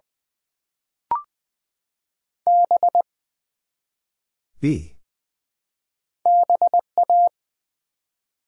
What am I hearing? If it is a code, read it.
Morse code, heard as a keyed tone, about 20 wpm, 700 Hz. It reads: B BA
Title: Build A Word - 1 Letter At A Time - 3 to 5-Letter Words - 20wpm